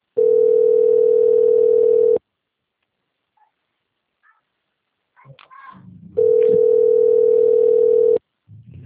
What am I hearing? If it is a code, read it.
Ringback tone (North America), 440 + 480 Hz.